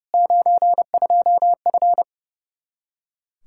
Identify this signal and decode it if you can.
Morse code, heard as a keyed tone, about 30 wpm, 700 Hz. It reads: 92F
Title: Practice QSO - Exchange 2 - Info - 30wpm